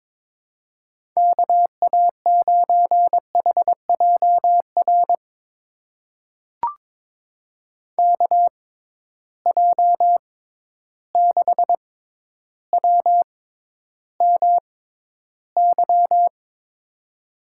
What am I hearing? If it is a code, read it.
Morse code, heard as a keyed tone, about 22 wpm, spaced as for 11 wpm, 700 Hz. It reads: KA9HJR KJ6WMY
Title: USA Call Signs - 6 Characters - 6x Character Spacing - 22wpm